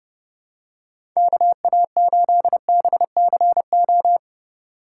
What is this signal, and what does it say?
Morse code, heard as a keyed tone, about 30 wpm, 700 Hz. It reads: KA8BCO
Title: USA Call Signs - 6 Characters - 8x Character Spacing - 30wpm